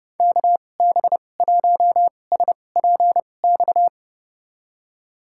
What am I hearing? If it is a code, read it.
Morse code, heard as a keyed tone, about 31 wpm, spaced as for 22 wpm, 700 Hz. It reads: KB1SPX
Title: USA Call Signs - 6 Characters - 2x Character Spacing - 30wpm